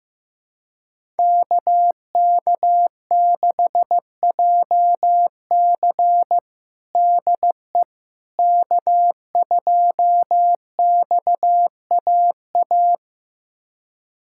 Morse code, heard as a keyed tone, about 15 wpm, 700 Hz. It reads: KK6JC DE K2XAA
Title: USA Callsign DE Callsign - 5 Characters - 15wpm